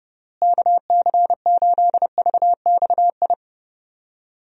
Morse code, heard as a keyed tone, about 30 wpm, 700 Hz. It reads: KC8VXI
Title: USA Call Signs - 6 Characters - Repeated 3x - 30wpm